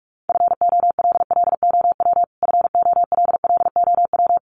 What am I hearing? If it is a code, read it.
Morse code, heard as a keyed tone, about 44 wpm, 700 Hz. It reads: FOLLOW FOLLOW
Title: Top 200 Words Repeated 45wpm